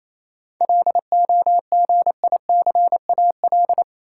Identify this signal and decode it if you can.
Morse code, heard as a keyed tone, about 27 wpm, 700 Hz. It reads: LOGICAL
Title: Seven Letter Words - 28wpm